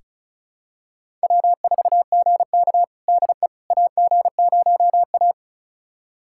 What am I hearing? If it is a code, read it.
Morse code, heard as a keyed tone, about 36 wpm, 700 Hz. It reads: W4GK DE AG0A
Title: USA Callsign DE Callsign - 4 Characters - 35wpm